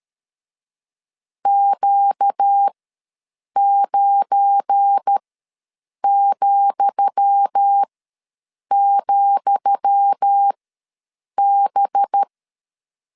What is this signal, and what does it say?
Morse code, heard as a keyed tone, about 13 wpm, spaced as for 7 wpm, 800 Hz. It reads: Q9,,B
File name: LEZIONI / 10 SIGNS / 10D COMA ws 12wpm 3618